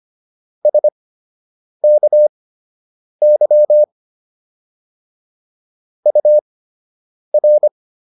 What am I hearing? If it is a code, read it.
Morse code, heard as a keyed tone, about 25 wpm, spaced as for 8 wpm, 600 Hz. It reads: SKY UR